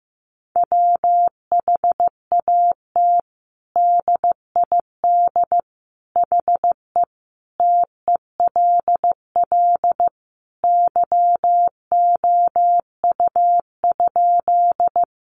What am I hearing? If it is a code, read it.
Morse code, heard as a keyed tone, about 15 wpm, 700 Hz. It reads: WHAT DID HE TELL YOU?
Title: Sentences from Top 200 Words 15wpm